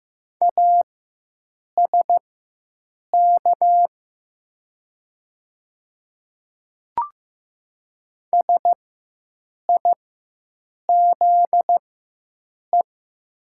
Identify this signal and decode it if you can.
Morse code, heard as a keyed tone, about 15 wpm, spaced as for 7 wpm, 700 Hz. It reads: ASK SIZE
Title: 4x Character Spacing - Top 500 Words - 15wpm